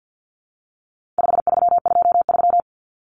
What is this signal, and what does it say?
Morse code, heard as a keyed tone, about 50 wpm, 700 Hz. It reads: 5323
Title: Numbers - 4-Digit 50wpm